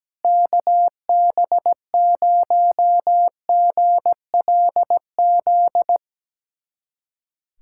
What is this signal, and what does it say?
Morse code, heard as a keyed tone, about 17 wpm, 700 Hz. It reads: KB0GLZ
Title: USA Callsign DE Callsign - 6 Characters - 17wpm